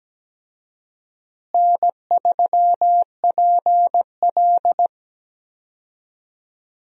Morse code, heard as a keyed tone, about 17 wpm, 700 Hz. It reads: N3PL
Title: USA Call Signs - 4 Characters - Repeated 3x - 17wpm